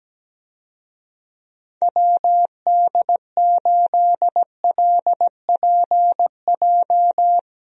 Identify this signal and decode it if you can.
Morse code, heard as a keyed tone, about 17 wpm, 700 Hz. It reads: WD8LPJ
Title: USA Call Signs - 6 Characters - 17wpm